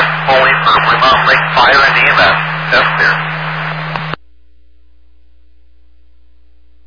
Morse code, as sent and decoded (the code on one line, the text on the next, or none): none